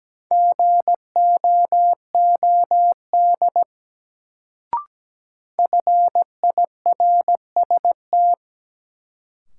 Morse code, heard as a keyed tone, about 17 wpm, 700 Hz. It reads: GOOD FIRST
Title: Sets of 2 Words - New Word Drill - 17wpm